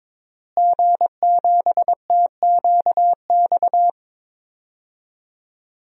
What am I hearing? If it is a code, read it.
Morse code, heard as a keyed tone, about 22 wpm, 700 Hz. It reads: G7TQX
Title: DXCC-100 Call Signs - 5 Characters - Repeated 3x - 22wpm